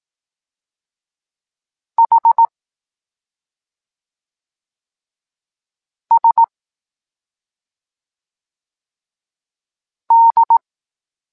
Morse code, heard as a keyed tone, about 18 wpm, 920 Hz. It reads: H S D
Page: Tekst 14 – doskonalenie odbioru poznanych liter